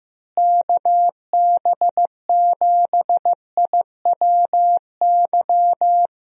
Morse code, heard as a keyed tone, about 15 wpm, 700 Hz. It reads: KB7IWY
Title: USA Call Signs - 6 Characters - Repeated 3x - 15wpm